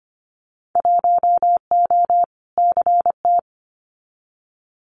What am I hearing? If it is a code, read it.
Morse code, heard as a keyed tone, about 25 wpm, 700 Hz. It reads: 1O CT
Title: Winter Field Day Exchanges 25wpm